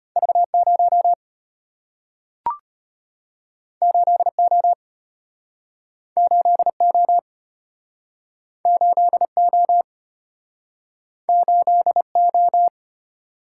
Morse code, heard as a keyed tone, about 34 wpm, 700 Hz. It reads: V0 8O 8O 8O 8O